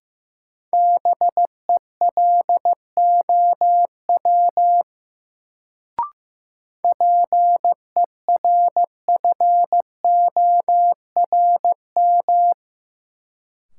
Morse code, heard as a keyed tone, about 15 wpm, 700 Hz. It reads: BELOW PERFORM
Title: Sentences from Top 1000 Words - Review Part 5 - New Word Drill - 15wpm